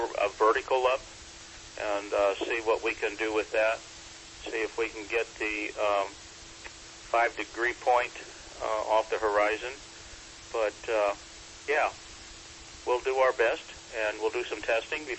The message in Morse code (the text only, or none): none